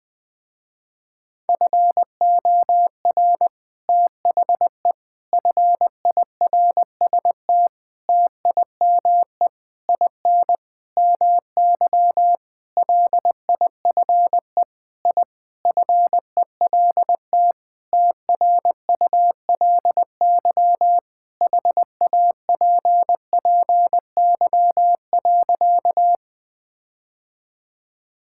Morse code, heard as a keyed tone, about 20 wpm, 700 Hz. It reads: FOR THE FIRST TIME IN MY LIFE I FELT TRULY HAPPY.
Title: Sentences with 7-Word Phrases - Spoken Beginning - 20wpm